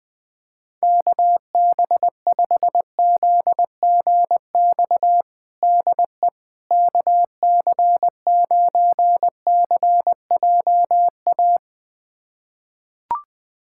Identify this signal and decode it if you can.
Morse code, heard as a keyed tone, about 21 wpm, 700 Hz. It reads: KB5ZGX DE KC9CJA
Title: USA Callsign DE Callsign - 6 Characters - 20wpm